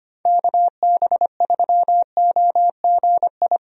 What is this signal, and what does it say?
Morse code, heard as a keyed tone, about 25 wpm, 700 Hz. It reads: KB3OGI
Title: USA Call Signs - 6 Characters - 25wpm